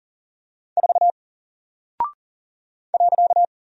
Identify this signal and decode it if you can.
Morse code, heard as a keyed tone, about 40 wpm, 700 Hz. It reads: 4 .
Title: Single Character (Letter:Number:Punctuation:Prosign) Alt 40wpm